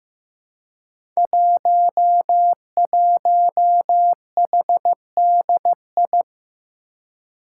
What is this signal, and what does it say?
Morse code, heard as a keyed tone, about 15 wpm, 700 Hz. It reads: 11HDI
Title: Five Challenging Letter-Number Combinations 15wpm